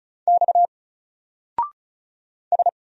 Morse code, heard as a keyed tone, about 36 wpm, 700 Hz. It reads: X S
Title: Single Character (Letter:Number:Punctuation:Prosign) Alt 35wpm